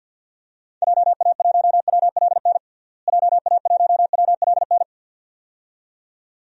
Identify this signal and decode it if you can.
Morse code, heard as a keyed tone, about 49 wpm, 700 Hz. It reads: JA1WLN JA1WLN
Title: DXCC-100 Call Signs - 6 Characters - Repeated 2x - 50wpm